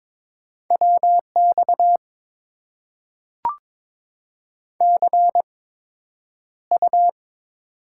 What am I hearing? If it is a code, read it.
Morse code, heard as a keyed tone, about 22 wpm, spaced as for 9 wpm, 700 Hz. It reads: WX CU
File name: Top 90 QSO Elements - 8x Character Spacing - 22wpm